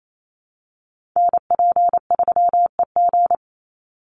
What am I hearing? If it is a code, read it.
Morse code, heard as a keyed tone, about 28 wpm, 700 Hz. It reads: NP3EG
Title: USA Call Signs 28wpm